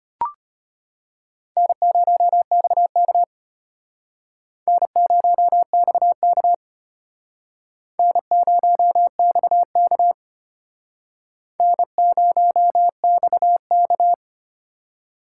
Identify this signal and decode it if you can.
Morse code, heard as a keyed tone, about 31 wpm, 700 Hz. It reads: N0XK N0XK N0XK N0XK
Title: Speed Racing - USA 1x2 Call Signs 25wpm